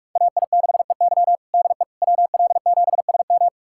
Morse code, heard as a keyed tone, about 46 wpm, 700 Hz. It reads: AI6EY DE WL7SM
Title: USA Callsign DE Callsign - 5 Characters - 45wpm